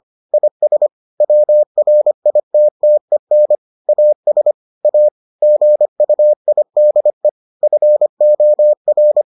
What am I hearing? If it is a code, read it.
Morse code, heard as a keyed tone, about 25 wpm, 600 Hz. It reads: IS WRITTEN AS A GUIDE FOR